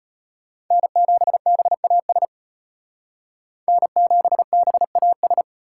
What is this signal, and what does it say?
Morse code, heard as a keyed tone, about 30 wpm, 700 Hz. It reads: N7BAS N7BAS
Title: Speed Racing - USA 1x3 Call Signs 25wpm